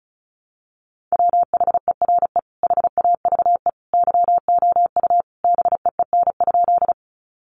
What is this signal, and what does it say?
Morse code, heard as a keyed tone, about 34 wpm, 700 Hz. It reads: WHERE HAVE YOU BEEN?
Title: Sentences from Top 200 Words 35wpm